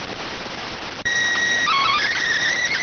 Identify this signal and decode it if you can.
SSTV (slow-scan television), PD 290.